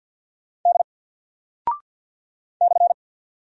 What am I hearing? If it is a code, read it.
Morse code, heard as a keyed tone, about 50 wpm, 700 Hz. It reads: D /
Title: Single Character (Letter:Number:Punctuation:Prosign) 50wpm